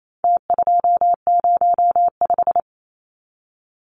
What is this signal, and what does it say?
Morse code, heard as a keyed tone, about 28 wpm, 700 Hz. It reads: T205
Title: Numbers - Contest Style - 4-Digit 28wpm